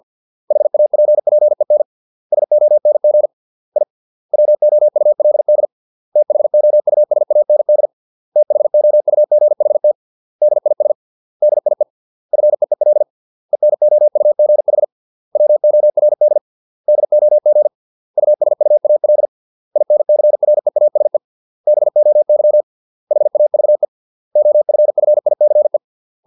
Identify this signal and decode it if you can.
Morse code, heard as a keyed tone, about 50 wpm, 600 Hz. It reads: HAPPEN SONG I WOULD THOUSAND THOUGHT DIS DIE FEEL ENOUGH WORD DOG USUAL INCREASE BOY HAVE OFFICE